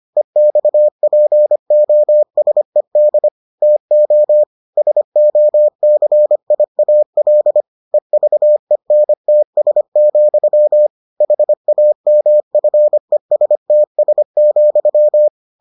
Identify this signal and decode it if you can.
Morse code, heard as a keyed tone, about 25 wpm, 600 Hz. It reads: EXPOSED TO SOCIAL EVENTS, HAMFESTS,